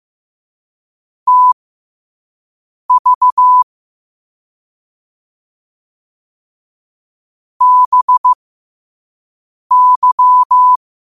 Morse code, heard as a keyed tone, about 15 wpm, spaced as for 5 wpm, 1000 Hz. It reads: TV BY
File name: qotdfiles / 5wpm